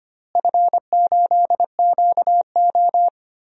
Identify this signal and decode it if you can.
Morse code, heard as a keyed tone, about 25 wpm, 700 Hz. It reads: F8QO
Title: DXCC-100 Call Signs - 4 Characters - Repeated 3x - 25wpm